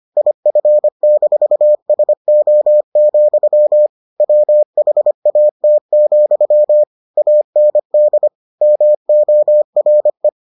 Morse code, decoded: IF-SO, WHAT, AND MORE

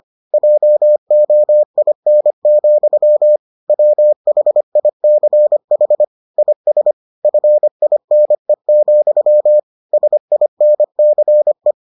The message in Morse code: JOIN, WHICH IS FINE, SINCE

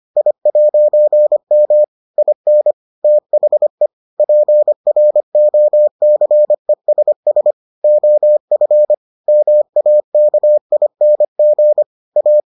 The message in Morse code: I'M IN THE PROCESS OF MAKING A

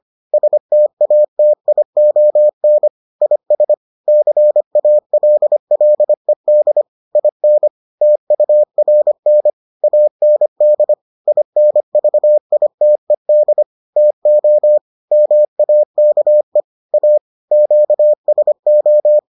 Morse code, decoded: STATION IS CALLED IN TURN AND INVITED TO MAKE A QSO

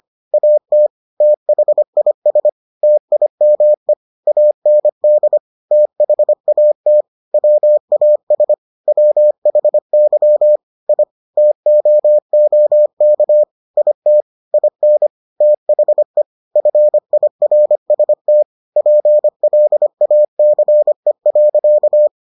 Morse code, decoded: AT THIS TIME AND THAT WAS WHY I TOOK IT IN THE FIRST PLACE.

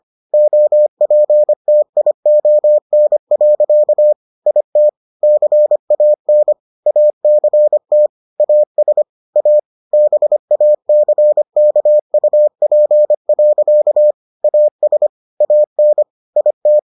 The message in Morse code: OPTION. IT CAN ACT AS A BACKUP. AS AN IT

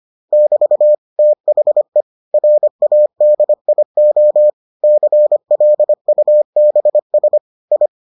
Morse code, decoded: <BT> THE RADIO CLUBS I

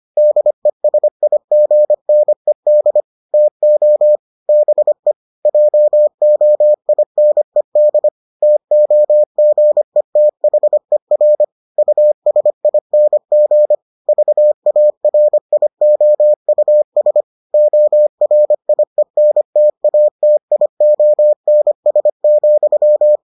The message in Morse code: DESIGNED TO BE JOINED TOGETHER USING VARIOUS ORIENTATIONS,

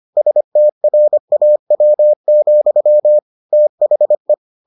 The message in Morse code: STRAW, THE

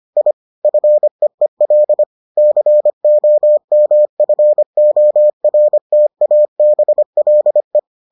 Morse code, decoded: I FEEL COMFORTABLE

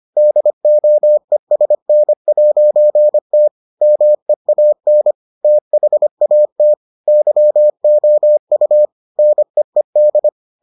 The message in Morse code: DOESN'T MEAN THAT YOU NEED

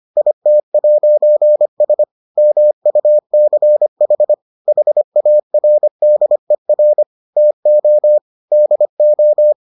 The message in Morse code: IT'S MUCH HARDER TO DO

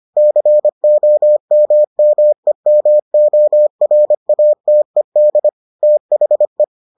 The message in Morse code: COMMEMORATED THE

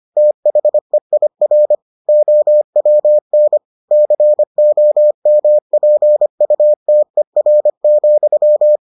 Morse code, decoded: THEIR OWN COMPUTER,

